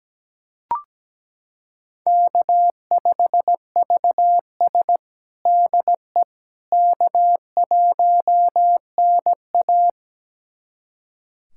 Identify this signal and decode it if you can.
Morse code, heard as a keyed tone, about 17 wpm, 700 Hz. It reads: K5VS DE K1NA
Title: USA Callsign DE Callsign - 4 Characters - 17wpm